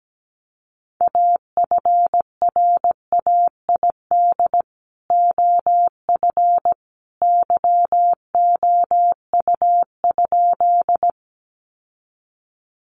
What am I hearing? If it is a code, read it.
Morse code, heard as a keyed tone, about 17 wpm, 700 Hz. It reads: AFRAID OF YOU?
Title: Sentences from Top 1500 Words 17wpm